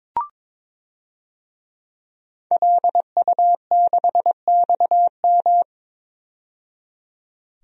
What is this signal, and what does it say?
Morse code, heard as a keyed tone, about 22 wpm, 700 Hz. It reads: LU6XM
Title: DXCC-100 Call Signs - 5 Characters - 22wpm